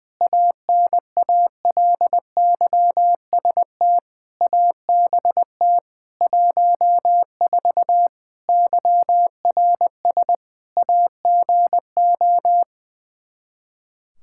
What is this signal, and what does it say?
Morse code, heard as a keyed tone, about 21 wpm, 700 Hz. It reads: ANALYST ABT 14 YRS AGO
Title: Practice QSO - Exchange 3 - More Info - 20wpm